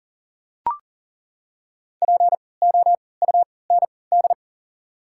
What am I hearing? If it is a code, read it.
Morse code, heard as a keyed tone, about 40 wpm, spaced as for 23 wpm, 700 Hz. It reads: POUND
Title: 3x Character Spacing - Top 500 Words - 40wpm